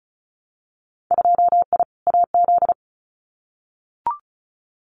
Morse code, heard as a keyed tone, about 34 wpm, 700 Hz. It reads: 2I AZ
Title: Winter Field Day Exchanges 35wpm